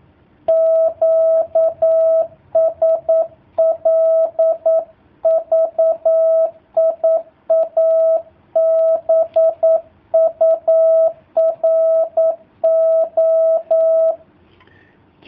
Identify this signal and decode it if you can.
Morse code, heard as a keyed tone, about 9 wpm, 650 Hz. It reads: QSL VIA BURO
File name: jh8cbh / 2019cw / sp09- w